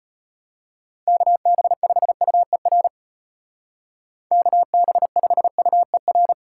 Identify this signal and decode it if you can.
Morse code, heard as a keyed tone, about 30 wpm, 700 Hz. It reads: KB5UER KB5UER
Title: Speed Racing - USA 2x3 Call Signs 25wpm